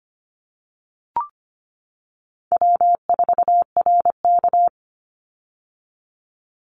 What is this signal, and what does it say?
Morse code, heard as a keyed tone, about 25 wpm, 700 Hz. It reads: W4RK
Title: Easy Call Signs - 25wpm